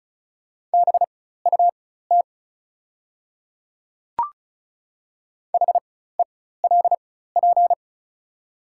Morse code, heard as a keyed tone, about 36 wpm, spaced as for 16 wpm, 690 Hz. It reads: BUT HELP
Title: 4x Character Spacing - Top 300 Words - 35wpm